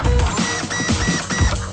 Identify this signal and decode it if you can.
There is no code.